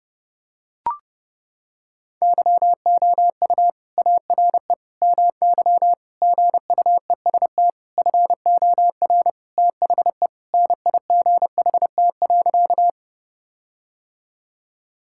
Morse code, heard as a keyed tone, about 30 wpm, 700 Hz. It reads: YOU ARE MY GUEST FOR THE NIGHT.
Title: Sentences from Top 1600 Words 30wpm